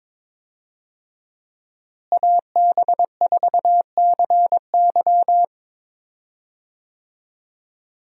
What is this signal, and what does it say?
Morse code, heard as a keyed tone, about 22 wpm, 700 Hz. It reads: AB4CY
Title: USA Call Signs - 5 Characters - Repeated 2x - 22wpm